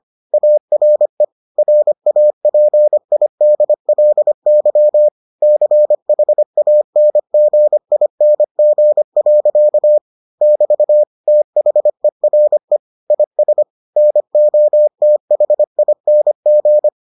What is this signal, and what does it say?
Morse code, heard as a keyed tone, about 25 wpm, 600 Hz. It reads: ARE RAPIDLY CHANGING. <BT> THERE IS NOTHING